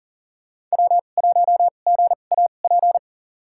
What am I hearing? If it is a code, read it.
Morse code, heard as a keyed tone, about 40 wpm, spaced as for 29 wpm, 700 Hz. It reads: W1GAP
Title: USA Call Signs - 5 Characters - 2x Character Spacing - 40wpm